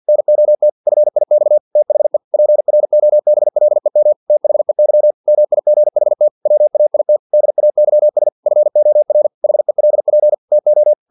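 Morse code, heard as a keyed tone, about 50 wpm, 600 Hz. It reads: NOT FIX THE PROBLEM THEY MIGHT WAIT DAYS FOR HELP TO
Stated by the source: Morse practice text from an English-language novel at 49 wpm.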